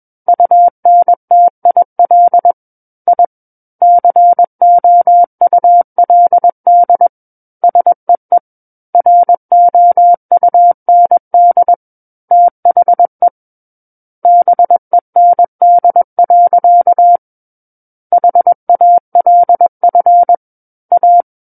Morse code, heard as a keyed tone, about 21 wpm, 700 Hz. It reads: UNTIL I COULD SEE ROUND THE BEND. HALF A
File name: Book One / 12-Chapter 12 - 21 WPM